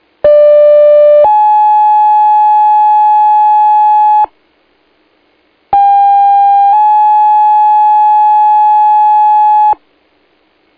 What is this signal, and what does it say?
Two-tone page, heard as a one-second tone, then a three-second tone, about 590 then 850 Hz, about 800 then 850 Hz. It is heard from 0.2 to 9.7 s.